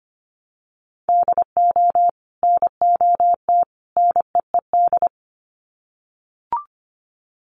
Morse code, heard as a keyed tone, about 25 wpm, 700 Hz. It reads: DO NOT NEED
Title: Sets of 3 Words 25wpm